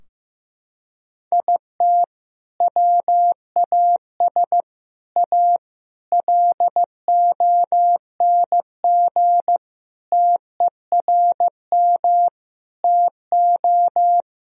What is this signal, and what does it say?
Morse code, heard as a keyed tone, about 15 wpm, 700 Hz. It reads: IT WAS A LONG TERM TO
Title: Sentences from Top 1000 Words - Review Part 4 - 15wpm